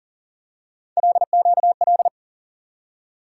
Morse code, heard as a keyed tone, about 40 wpm, 700 Hz. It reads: LQL
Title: USA Call Sign Postfixes - 3 Letters 40wpm